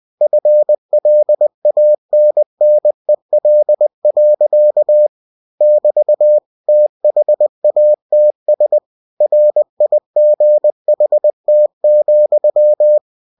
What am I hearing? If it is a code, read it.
Morse code, heard as a keyed tone, about 20 wpm, 600 Hz. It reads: FLANNEL. <BT> THATS RIGHT,